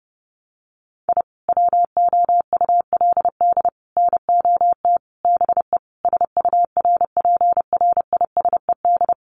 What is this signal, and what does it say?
Morse code, heard as a keyed tone, about 30 wpm, 700 Hz. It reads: I WOULD NOT BE SURPRISED
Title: Sets of 5 Words Encore2 30wpm